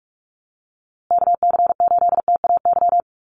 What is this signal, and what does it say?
Morse code, heard as a keyed tone, about 44 wpm, 700 Hz. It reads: KC8TAY
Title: USA Call Signs Repeated 2x 45wpm